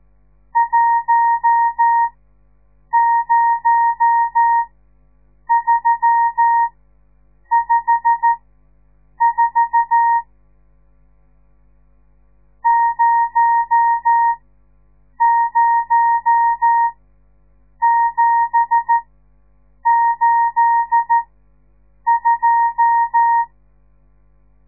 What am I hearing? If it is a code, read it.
Morse code, heard as a keyed tone, about 13 wpm, spaced as for 7 wpm, 930 Hz. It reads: 10354 00782
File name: radiomorningstar / morse / les 10